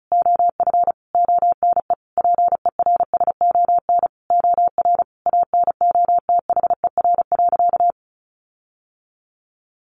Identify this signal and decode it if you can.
Morse code, heard as a keyed tone, about 34 wpm, 700 Hz. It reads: OF ONE PERSON OR ANOTHER.
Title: Sentences from Top 1400 Words 35wpm